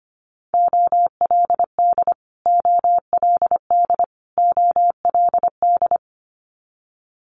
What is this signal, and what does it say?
Morse code, heard as a keyed tone, about 25 wpm, 700 Hz. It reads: OLD OLD OLD
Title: Multi-letter Phonemes Repeated 3x 25wpm